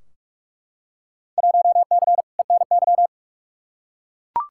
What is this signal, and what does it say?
Morse code, heard as a keyed tone, about 41 wpm, 700 Hz. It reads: 1C ENY
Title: Field Day Exchanges Encore - 45wpm